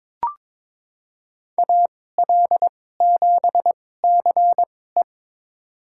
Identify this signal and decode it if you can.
Morse code, heard as a keyed tone, about 21 wpm, spaced as for 16 wpm, 700 Hz. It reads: AL7CE
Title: USA Call Signs - 2x Character Spacing - 22wpm